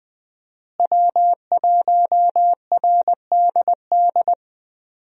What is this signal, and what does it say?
Morse code, heard as a keyed tone, about 20 wpm, 700 Hz. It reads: W1RDD
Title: USA Call Signs - 5 Characters - Repeated 3x - 20wpm